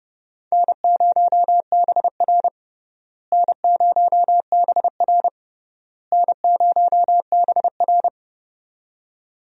Morse code, heard as a keyed tone, about 30 wpm, 700 Hz. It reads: N0BR N0BR N0BR
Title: USA Call Signs - 4 Characters - Repeated 3x - 30wpm